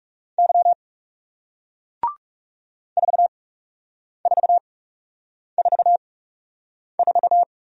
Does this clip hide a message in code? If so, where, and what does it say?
Yes: Morse code from 0.4 to 7.4 s, heard as a keyed tone, about 38 wpm, spaced as for 8 wpm, 700 Hz. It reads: Y 4444